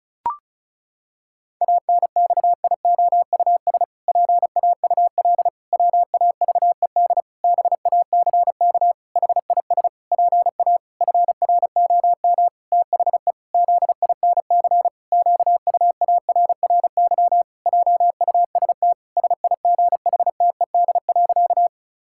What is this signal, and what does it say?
Morse code, heard as a keyed tone, about 36 wpm, 700 Hz. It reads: ANXIOUS PAUL WAVED BACK HIS PA FROM THE ZINC QUARRY JUST SIGHTED.
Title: Pangram Challenge v2 - 35wpm